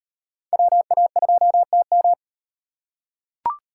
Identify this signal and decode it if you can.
Morse code, heard as a keyed tone, about 38 wpm, 700 Hz. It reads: WA2TM